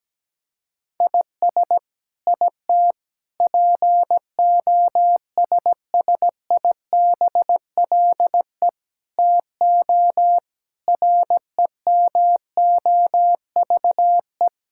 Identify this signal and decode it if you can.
Morse code, heard as a keyed tone, about 17 wpm, 700 Hz. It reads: IS IT POSSIBLE TO REMOVE
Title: Sentences from Top 1000 Words - Review Part 5 - 17wpm